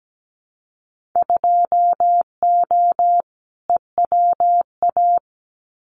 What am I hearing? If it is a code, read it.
Morse code, heard as a keyed tone, about 17 wpm, 700 Hz. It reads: 2O EWA
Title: Winter Field Day Exchanges Encore 17wpm